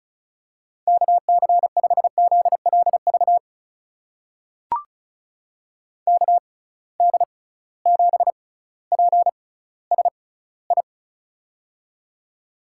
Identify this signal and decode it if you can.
Morse code, heard as a keyed tone, about 34 wpm, spaced as for 18 wpm, 700 Hz. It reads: KC5ZLV KD7PSI
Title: USA Call Signs - 6 Characters - 6x Character Spacing - 35wpm